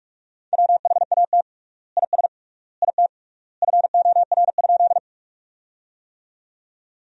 Morse code, heard as a keyed tone, about 46 wpm, 700 Hz. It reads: WHAT IS IT FOR?
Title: 3x Word-Spacing - Sentences from Top 100 Words - 45wpm